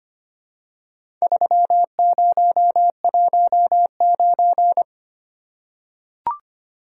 Morse code, heard as a keyed tone, about 25 wpm, 700 Hz. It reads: 3019
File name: practice / 25wpm / Four Letter-Number Combinations - 25wpm